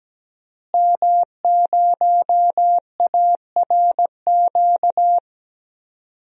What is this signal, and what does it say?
Morse code, heard as a keyed tone, about 17 wpm, 700 Hz. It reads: M0ARQ
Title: DXCC-100 Call Signs - 5 Characters - Repeated 2x - 17wpm